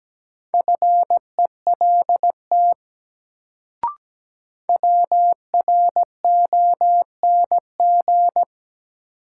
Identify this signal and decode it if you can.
Morse code, heard as a keyed tone, about 17 wpm, 700 Hz. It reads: FELT WRONG
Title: Sets of 5 Words Encore - New Word Drill - 17wpm